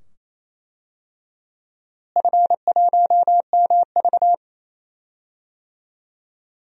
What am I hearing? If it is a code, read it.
Morse code, heard as a keyed tone, about 27 wpm, 700 Hz. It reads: F1MV
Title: DXCC-100 Call Signs - 4 Characters - Repeated 2x - 28wpm